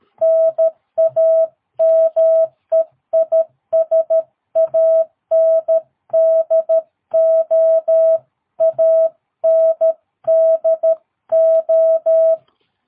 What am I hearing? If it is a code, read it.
Morse code, heard as a keyed tone, about 13 wpm, 650 Hz. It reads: NAME IS ANDO ANDO
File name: jh8cbh / 2022cw